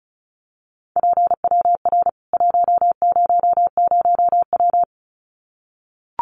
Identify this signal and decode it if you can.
Morse code, heard as a keyed tone, about 34 wpm, 700 Hz. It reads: PWR 100W
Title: Common QSO Phrases 35wpm